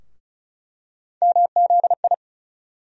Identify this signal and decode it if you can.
Morse code, heard as a keyed tone, about 34 wpm, 700 Hz. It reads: MZI
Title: USA Call Sign Postfixes - 3 Letters 35wpm